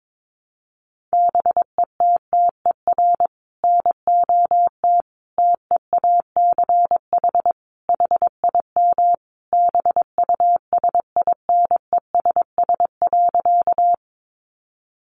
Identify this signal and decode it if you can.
Morse code, heard as a keyed tone, about 22 wpm, 700 Hz. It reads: BETTER NOT TEACH HIM BUSINESS.
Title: Sentences from Top 600 Words 22wpm